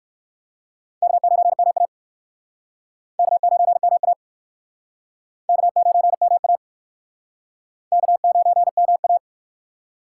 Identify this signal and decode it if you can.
Morse code, heard as a keyed tone, about 56 wpm, 700 Hz. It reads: K9MA K9MA K9MA K9MA
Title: Speed Racing - USA Call Signs 45wpm